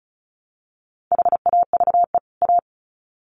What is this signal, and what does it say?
Morse code, heard as a keyed tone, about 34 wpm, 700 Hz. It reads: HAVE A
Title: Sets of 2 Words 35wpm